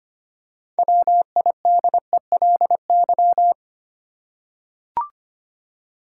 Morse code, heard as a keyed tone, about 25 wpm, 700 Hz. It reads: WIDELY
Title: Six Letter Words - 25wpm